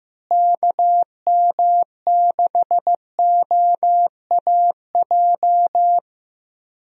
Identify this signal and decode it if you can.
Morse code, heard as a keyed tone, about 15 wpm, 700 Hz. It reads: KM6OAJ
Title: USA Call Signs - 6 Characters - Repeated 3x - 15wpm